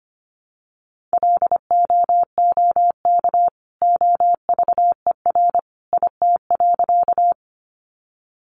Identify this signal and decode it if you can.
Morse code, heard as a keyed tone, about 25 wpm, 700 Hz. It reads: LOOK OVER IT.
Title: Sentences from Top 100 Words 25wpm